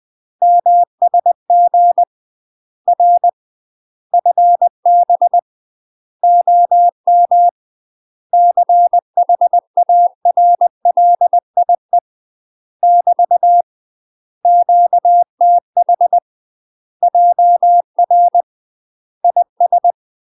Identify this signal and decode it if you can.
Morse code, heard as a keyed tone, about 20 wpm, 700 Hz. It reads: MSG R FB OM CHARLIE <BT> QTH JR IS
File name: UK MASTER QSOs TRAINING / 20WPM / UK MASTER QSOs 0060